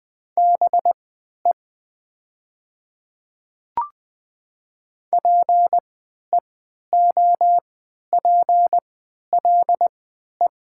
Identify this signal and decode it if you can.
Morse code, heard as a keyed tone, about 20 wpm, spaced as for 11 wpm, 700 Hz. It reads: BE PEOPLE
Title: 3x Character Spacing - Top 100 Words - 20wpm